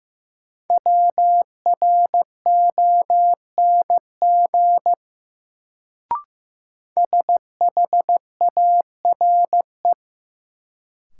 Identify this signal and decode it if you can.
Morse code, heard as a keyed tone, about 15 wpm, 700 Hz. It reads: WRONG SHARE
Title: Sentences from Top 900 Words - New Word Drill - 15wpm